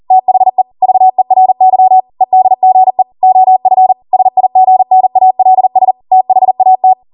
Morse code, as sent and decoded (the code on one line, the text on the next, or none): - .... / ...- . .-. -.-- / . -.. --. / --- ..-. / ... .. --. -. .- .-.. ... / - .... .- -
THE VERY EDGE OF SIGNALS THAT